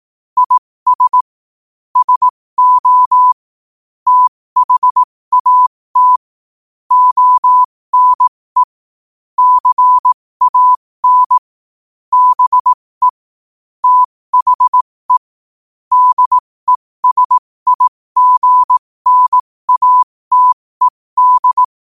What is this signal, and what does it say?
Morse code, heard as a keyed tone, about 18 wpm, spaced as for 15 wpm, 1000 Hz. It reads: IS SO THAT ONE CAN BE THE DESIGNATED
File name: qotdfiles / 15wpm